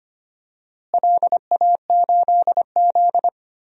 Morse code, heard as a keyed tone, about 25 wpm, 700 Hz. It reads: LA8Z
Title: DXCC-100 Call Signs - 4 Characters - Repeated 2x - 25wpm